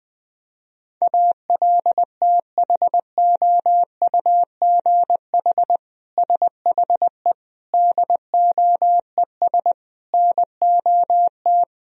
Morse code, heard as a keyed tone, about 20 wpm, 700 Hz. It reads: ALTHOUGH SHE DOES NOT
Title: Sentences from Top 1000 Words - Review Part 3 - 20wpm